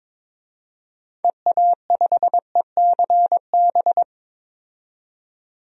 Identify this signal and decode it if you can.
Morse code, heard as a keyed tone, about 21 wpm, 700 Hz. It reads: EA5ECB
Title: DXCC-100 Call Signs - 6 Characters - 22wpm